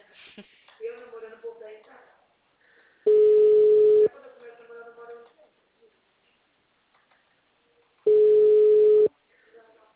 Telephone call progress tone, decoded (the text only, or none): ringback tone